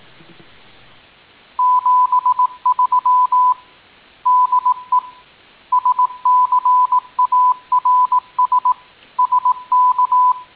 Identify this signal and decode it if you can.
Morse code, heard as a keyed tone, about 18 wpm, 1000 Hz. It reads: 73 DE SCARS SK